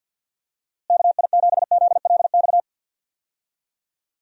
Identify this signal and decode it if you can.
Morse code, heard as a keyed tone, about 47 wpm, 700 Hz. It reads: KI7ZLX
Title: USA Call Signs - 6 Characters - Repeated 2x - 50wpm